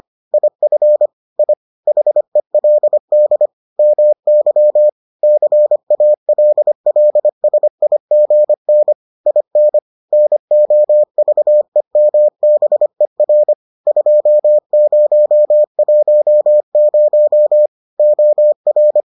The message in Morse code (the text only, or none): IF I HELD MY CALLSIGN IN NOVEMBER 2010 OR